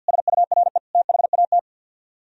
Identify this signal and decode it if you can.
Morse code, heard as a keyed tone, about 64 wpm, 700 Hz. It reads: SURE THAT